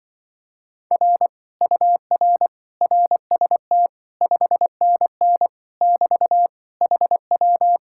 Morse code, heard as a keyed tone, about 23 wpm, 700 Hz. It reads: R UR RST 5NN <BT> HW
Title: Speed-Racing Common QSO Phrases - 20wpm